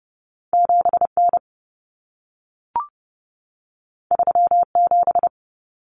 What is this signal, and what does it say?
Morse code, heard as a keyed tone, about 30 wpm, 700 Hz. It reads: 7N 37
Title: Common Cut Numbers - 2-Digit 30wpm